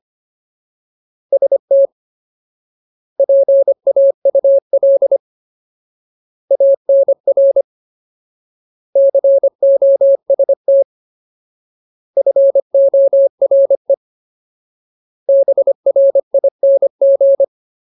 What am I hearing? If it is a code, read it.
Morse code, heard as a keyed tone, about 25 wpm, 550 Hz. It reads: ST PAUL ANR COST FORE BRING
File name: Ultimate Word List / 25WPM Ultimate